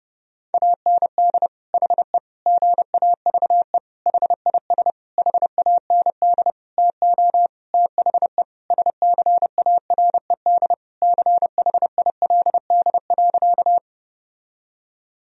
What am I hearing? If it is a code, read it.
Morse code, heard as a keyed tone, about 30 wpm, 700 Hz. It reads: AND HE GAVE HIS HAND TO THE SCARED CHILD.
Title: Sentences from Top 1000 Words - Review Part 4 - 30wpm